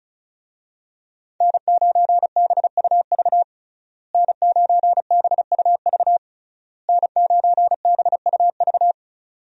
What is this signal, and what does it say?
Morse code, heard as a keyed tone, about 34 wpm, 700 Hz. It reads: N9BUV N9BUV N9BUV
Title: USA Call Signs - 5 Characters - Repeated 3x - 35wpm